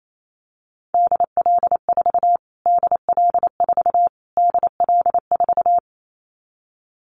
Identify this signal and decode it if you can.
Morse code, heard as a keyed tone, about 28 wpm, 700 Hz. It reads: DL4 DL4 DL4
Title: Callsign Prefixes for Top 25 DXCC - Repeated 3x - 28wpm